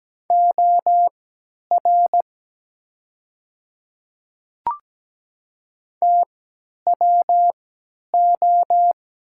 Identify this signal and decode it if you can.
Morse code, heard as a keyed tone, about 17 wpm, spaced as for 10 wpm, 700 Hz. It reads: OR TWO
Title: 3x Character Spacing - Top 100 Words - 17wpm